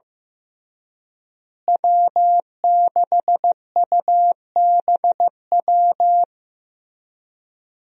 Morse code, heard as a keyed tone, about 15 wpm, 700 Hz. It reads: W6UBW
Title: USA Call Signs - 5 Characters - Repeated 3x - 15wpm